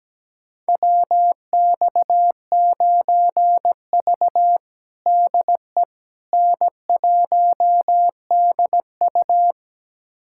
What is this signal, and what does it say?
Morse code, heard as a keyed tone, about 17 wpm, 700 Hz. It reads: WX9V DE N1DU
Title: USA Callsign DE Callsign - 4 Characters - 17wpm